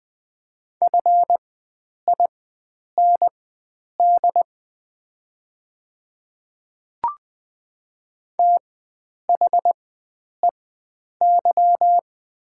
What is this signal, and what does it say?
Morse code, heard as a keyed tone, about 20 wpm, spaced as for 9 wpm, 700 Hz. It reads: FIND THEY
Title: 4x Character Spacing - Top 300 Words - 20wpm